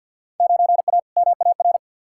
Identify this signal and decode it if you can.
Morse code, heard as a keyed tone, about 47 wpm, 700 Hz. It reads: 9A MAR
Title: Field Day Exchanges Encore - 50wpm